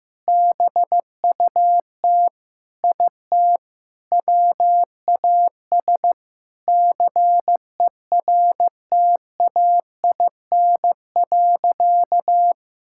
Morse code, heard as a keyed tone, about 15 wpm, 700 Hz. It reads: BUT IT WAS CERTAIN.
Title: Sentences from Top 1000 Words - Review Part 6 - 15wpm